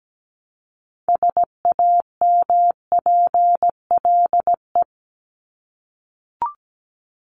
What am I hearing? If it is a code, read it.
Morse code, heard as a keyed tone, about 17 wpm, 700 Hz. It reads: SAMPLE
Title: Top 1501-1750 Words 17wpm